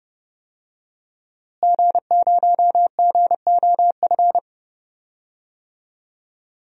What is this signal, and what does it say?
Morse code, heard as a keyed tone, about 30 wpm, 700 Hz. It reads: G0GOF
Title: DXCC-100 Call Signs - 5 Characters - Repeated 2x - 30wpm